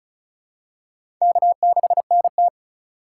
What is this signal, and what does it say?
Morse code, heard as a keyed tone, about 36 wpm, 700 Hz. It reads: K6NT
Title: USA Call Signs - 4 Characters - 3x Character Spacing - 35wpm